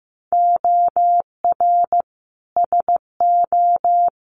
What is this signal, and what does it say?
Morse code, heard as a keyed tone, about 15 wpm, 700 Hz. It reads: OR SO
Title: Sentences from Top 100 Words 15wpm